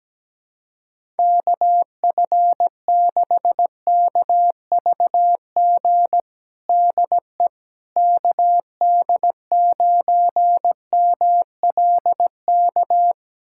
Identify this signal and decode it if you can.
Morse code, heard as a keyed tone, about 17 wpm, 700 Hz. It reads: KF6KVG DE KD9MLK
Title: USA Callsign DE Callsign 17wpm